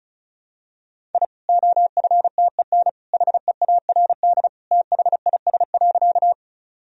Morse code, heard as a keyed tone, about 36 wpm, 700 Hz. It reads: I OFTEN HEARD THIS.
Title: Sentences from Top 1000 Words - Review Part 1 - 35wpm